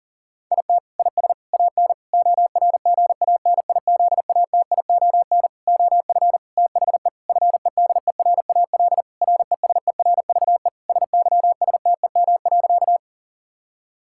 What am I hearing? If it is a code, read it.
Morse code, heard as a keyed tone, about 40 wpm, 700 Hz. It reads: IT IS AN ORGANIZATION OF THE FEDERAL RESERVE SYSTEM.
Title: Sentences from Top 1000 Words - Review Part 6 - 40wpm